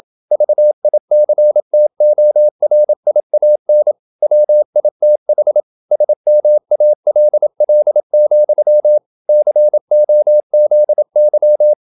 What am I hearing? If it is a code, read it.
Morse code, heard as a keyed tone, about 27 wpm, 600 Hz. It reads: VICTORIAN WITH SMALL, COZY
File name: Never 0027wpm